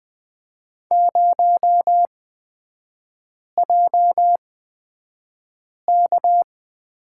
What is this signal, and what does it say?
Morse code, heard as a keyed tone, about 20 wpm, spaced as for 5 wpm, 700 Hz. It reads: 0JK
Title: 250 Lesson 40e - BK - Three Characters 20wpm